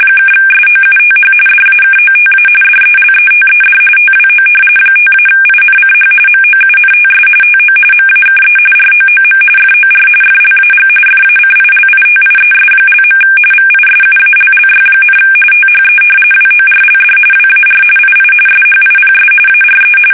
RTTY, about 45.45 baud. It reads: (SOUTH DAKOTA DELETED TWO 5-INCH MOUNTS TO MAKE ROOM FOR FLAG FACILITIES). VISUALLY, THE WORLD WAR II SHIPS ARE
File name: fsk850 / battleships